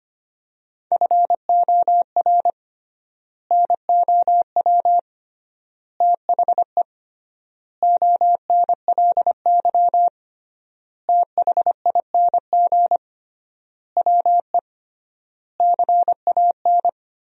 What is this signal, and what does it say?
Morse code, heard as a keyed tone, about 25 wpm, 700 Hz. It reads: FOR NOW THE ONLY THING WE CAN
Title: 3x Word-Spacing - Sentences from Top 200 Words - 25wpm